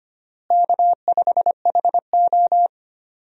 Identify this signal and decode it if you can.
Morse code, heard as a keyed tone, about 25 wpm, 700 Hz. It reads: K5HO
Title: USA Call Signs - 4 Characters - Repeated 3x - 25wpm